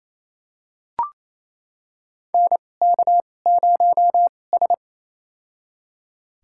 Morse code, heard as a keyed tone, about 28 wpm, spaced as for 20 wpm, 700 Hz. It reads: NK0S
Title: USA Call Signs - 4 Characters - 2x Character Spacing - 28wpm